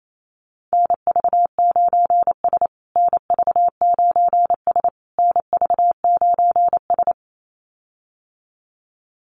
Morse code, heard as a keyed tone, about 28 wpm, 700 Hz. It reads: NV9S NV9S NV9S
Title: USA Call Signs Repeated 3x 28wpm